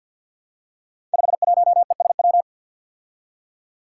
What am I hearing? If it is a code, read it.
Morse code, heard as a keyed tone, about 53 wpm, 700 Hz. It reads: 51ESW
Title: Five Challenging Letter-Number Combinations 50wpm